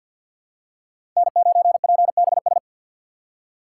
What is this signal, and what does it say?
Morse code, heard as a keyed tone, about 53 wpm, 700 Hz. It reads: N9PBS
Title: USA Call Signs - 5 Characters - 6x Character Spacing - 50wpm